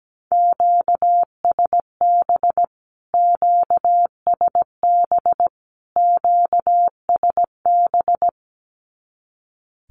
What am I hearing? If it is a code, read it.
Morse code, heard as a keyed tone, about 17 wpm, 700 Hz. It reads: QSB QSB QSB
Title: Q-Codes Repeated 3x 17wpm